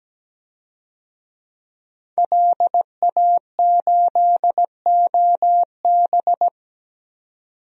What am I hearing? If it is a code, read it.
Morse code, heard as a keyed tone, about 17 wpm, 700 Hz. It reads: LA8OB
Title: DXCC-100 Call Signs - 5 Characters - 17wpm